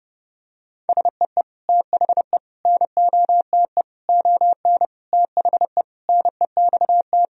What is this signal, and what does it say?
Morse code, heard as a keyed tone, about 30 wpm, 700 Hz. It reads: SEE THE NOTE ON THE NEXT
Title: Sentences from Top 1000 Words - Review Part 2 - 30wpm